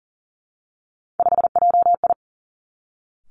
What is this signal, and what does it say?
Morse code, heard as a keyed tone, about 39 wpm, 700 Hz. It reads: 5JI
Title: Three Challenging Letter-Number Combinations 40wpm